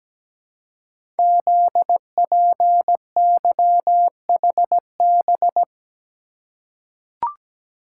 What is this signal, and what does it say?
Morse code, heard as a keyed tone, about 17 wpm, 700 Hz. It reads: ZPYHB
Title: Code Groups - 5 Letter:Numbers - 17wpm